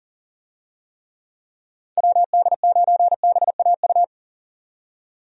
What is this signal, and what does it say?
Morse code, heard as a keyed tone, about 40 wpm, 700 Hz. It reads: WD9BAU
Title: USA Call Signs - 6 Characters - 40wpm